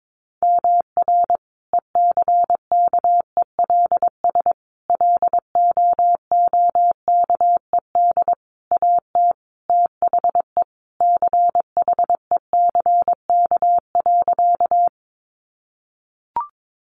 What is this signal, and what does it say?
Morse code, heard as a keyed tone, about 22 wpm, 700 Hz. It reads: MR ECKELS LOOKED AT THE CHECK.
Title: A Sound Of Thunder 22wpm